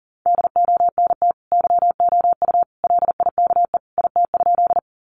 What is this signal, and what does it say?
Morse code, heard as a keyed tone, about 40 wpm, 700 Hz. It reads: DONT YOU LIKE IT?
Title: Sentences from Top 300 Words 40wpm